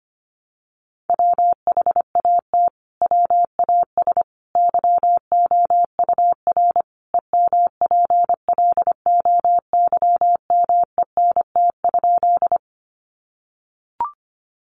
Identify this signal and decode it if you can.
Morse code, heard as a keyed tone, about 25 wpm, 700 Hz. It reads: WHAT WAS YOUR EMPLOYMENT?
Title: Sentences from Top 1500 Words 25wpm